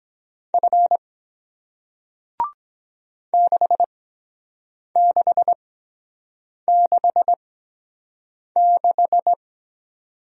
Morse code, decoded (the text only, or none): F 6666